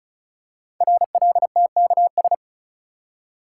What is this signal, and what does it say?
Morse code, heard as a keyed tone, about 36 wpm, 690 Hz. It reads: RLTKS
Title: Code Groups - 5 Letters - 35wpm